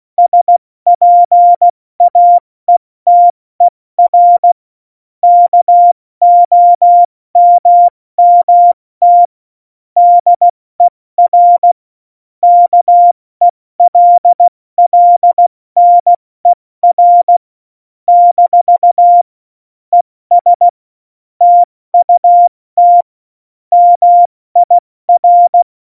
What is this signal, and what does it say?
Morse code, heard as a keyed tone, about 16 wpm, 700 Hz. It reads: SPAETER KOMMT DER KELLNER - ES TUT MIR